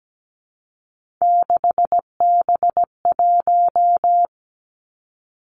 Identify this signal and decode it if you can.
Morse code, heard as a keyed tone, about 17 wpm, 700 Hz. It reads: 6B1
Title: Three Challenging Letter-Number Combinations 17wpm